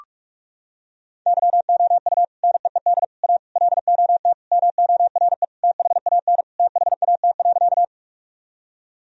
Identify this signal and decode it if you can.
Morse code, heard as a keyed tone, about 43 wpm, 700 Hz. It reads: YOU NEED A LOT MORE THAN THAT.
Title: Sentences from Top 1000 Words - Review Part 3 - 45wpm